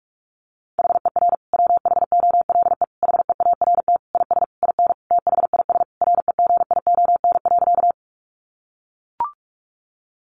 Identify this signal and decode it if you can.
Morse code, heard as a keyed tone, about 44 wpm, 700 Hz. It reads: HER WHOLE HEART IS IN THIS REGION.
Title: Sentences from Top 1500 Words 45wpm